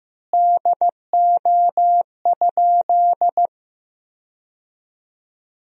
Morse code, heard as a keyed tone, about 15 wpm, 700 Hz. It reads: DO?
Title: 3x Word-Spacing - Sentences from Top 100 Words - 15wpm